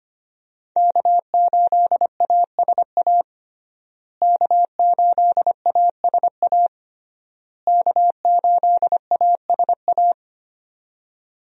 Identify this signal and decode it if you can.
Morse code, heard as a keyed tone, about 25 wpm, 700 Hz. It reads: K8ASA K8ASA K8ASA
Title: USA Call Signs - 5 Characters - Repeated 3x - 25wpm